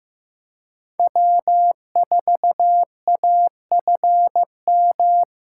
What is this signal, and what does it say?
Morse code, heard as a keyed tone, about 15 wpm, 700 Hz. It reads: W4AFM
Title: USA Call Signs - 5 Characters - 2x Character Spacing - 15wpm